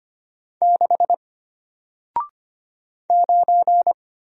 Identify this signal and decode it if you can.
Morse code, heard as a keyed tone, about 25 wpm, 700 Hz. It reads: 6 9
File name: Single-Letter-Number-25wpm